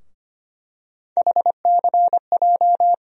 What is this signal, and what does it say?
Morse code, heard as a keyed tone, about 25 wpm, 700 Hz. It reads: HCJ
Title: USA Call Sign Postfixes - 3 Letters 25wpm